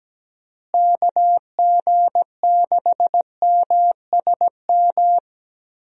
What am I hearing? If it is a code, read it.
Morse code, heard as a keyed tone, about 17 wpm, 700 Hz. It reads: KG6MSM
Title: USA Call Signs - 6 Characters - Repeated 3x - 17wpm